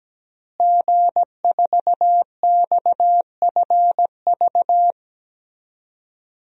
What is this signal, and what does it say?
Morse code, heard as a keyed tone, about 17 wpm, 700 Hz. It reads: G4XFV
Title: DXCC-100 Call Signs - 5 Characters - Repeated 3x - 17wpm